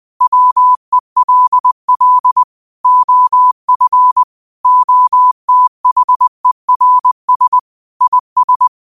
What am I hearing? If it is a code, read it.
Morse code, heard as a keyed tone, about 20 wpm, 1000 Hz. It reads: WELL OF OTHERS IS
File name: qotdfiles / 20wpm